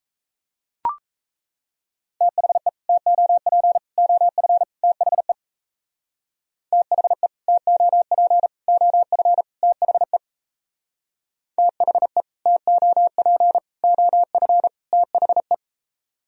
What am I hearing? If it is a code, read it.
Morse code, heard as a keyed tone, about 37 wpm, 700 Hz. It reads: THE TOP OF THE THE TOP OF THE THE TOP OF THE